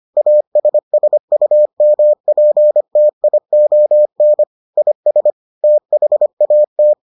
Morse code, heard as a keyed tone, about 25 wpm, 600 Hz. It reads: ASSUMPTION IS THAT